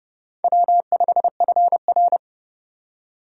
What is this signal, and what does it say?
Morse code, heard as a keyed tone, about 27 wpm, 700 Hz. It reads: W5FR